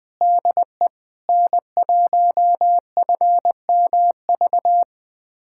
Morse code, heard as a keyed tone, about 20 wpm, 700 Hz. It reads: DE N1FMV
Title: USA Callsign DE Callsign 20wpm